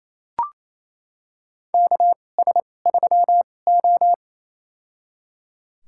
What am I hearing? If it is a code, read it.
Morse code, heard as a keyed tone, about 28 wpm, spaced as for 20 wpm, 700 Hz. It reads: KS3O
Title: USA Call Signs - 2x Character Spacing - 28wpm